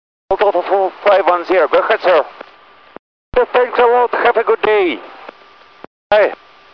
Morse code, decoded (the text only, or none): none